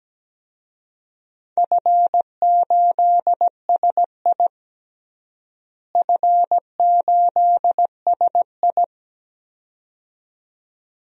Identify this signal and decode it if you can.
Morse code, heard as a keyed tone, about 17 wpm, 700 Hz. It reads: F8SI F8SI
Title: DXCC-100 Call Signs - 4 Characters - Repeated 2x - 17wpm